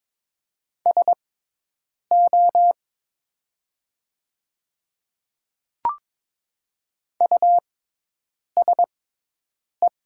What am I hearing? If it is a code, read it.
Morse code, heard as a keyed tone, about 22 wpm, spaced as for 8 wpm, 700 Hz. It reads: SO USE
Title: 6x Character Spacing - Top 200 Words - 22wpm